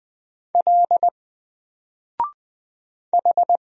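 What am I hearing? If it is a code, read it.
Morse code, heard as a keyed tone, about 20 wpm, 700 Hz. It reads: L H
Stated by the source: Single letter and number Morse drill at 20 wpm, from N4LCD.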